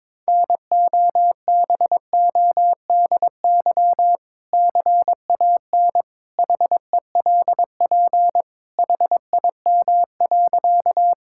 Morse code, heard as a keyed tone, about 22 wpm, 700 Hz. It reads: NOBODY CAN HELP HIM.
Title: Sentences from Top 1600 Words 22wpm